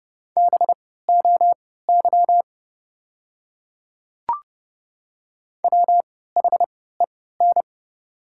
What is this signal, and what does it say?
Morse code, heard as a keyed tone, about 31 wpm, spaced as for 17 wpm, 690 Hz. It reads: BOY WHEN
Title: 3x Character Spacing - Top 300 Words - 30wpm